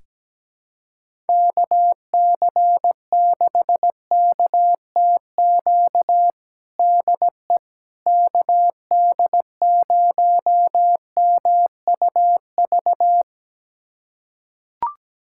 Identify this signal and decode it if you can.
Morse code, heard as a keyed tone, about 17 wpm, 700 Hz. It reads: KC6KTQ DE KD0MUV
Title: USA Callsign DE Callsign 17wpm